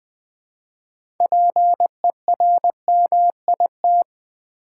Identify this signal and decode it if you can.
Morse code, heard as a keyed tone, about 20 wpm, 700 Hz. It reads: PERMIT
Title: Six Letter Words - 20wpm